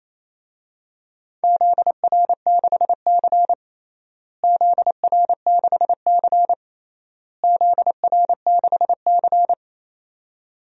Morse code, heard as a keyed tone, about 28 wpm, 700 Hz. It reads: ZR6C ZR6C ZR6C
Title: DXCC-100 Call Signs - 4 Characters - Repeated 3x - 28wpm